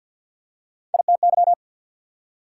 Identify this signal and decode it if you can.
Morse code, heard as a keyed tone, about 54 wpm, 690 Hz. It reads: ITY
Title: Prefixes and Suffixes 50wpm